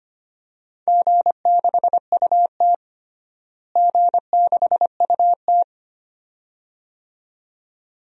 Morse code, heard as a keyed tone, about 25 wpm, 700 Hz. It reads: G6UT G6UT
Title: DXCC-100 Call Signs - 4 Characters - Repeated 2x - 25wpm